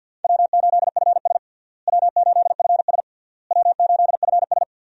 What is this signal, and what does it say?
Morse code, heard as a keyed tone, about 50 wpm, 700 Hz. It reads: W8FS W8FS W8FS
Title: USA Call Signs - 4 Characters - Repeated 3x - 50wpm